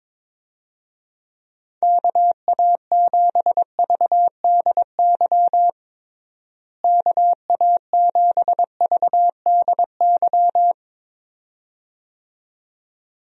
Morse code, heard as a keyed tone, about 22 wpm, 700 Hz. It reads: KA7VDY KA7VDY
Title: USA Call Signs - 6 Characters - Repeated 2x - 22wpm